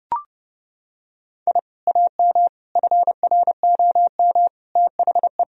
Morse code, heard as a keyed tone, about 32 wpm, 700 Hz. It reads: I AM FROM THE
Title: Sentences from Top 1000 Words - Review Part 1 - 30wpm